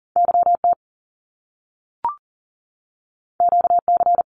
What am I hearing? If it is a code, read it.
Morse code, heard as a keyed tone, about 38 wpm, 700 Hz. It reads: YT QC
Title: Canadian Provinces and Territories 40wpm